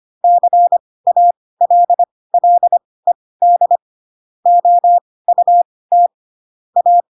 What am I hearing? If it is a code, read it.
Morse code, heard as a keyed tone, about 25 wpm, spaced as for 18 wpm, 700 Hz. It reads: CALLED OUT A
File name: Pinocchio ch3 12 25 ik1hge